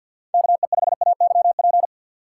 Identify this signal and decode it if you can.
Morse code, heard as a keyed tone, about 53 wpm, 700 Hz. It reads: KE5AYP
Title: USA Call Signs - 4x Character Spacing - 50wpm